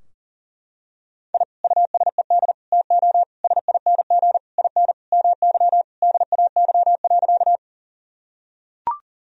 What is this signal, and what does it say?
Morse code, heard as a keyed tone, about 40 wpm, 700 Hz. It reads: I USED TO SING IN MY DAY.
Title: Sentences from Top 1000 Words - Review Part 2 - 40wpm